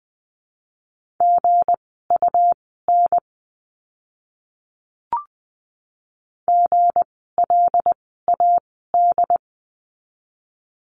Morse code, heard as a keyed tone, about 20 wpm, spaced as for 14 wpm, 700 Hz. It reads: GUN GLAD
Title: 160 Lesson 26f - Letter g - Words 20wpm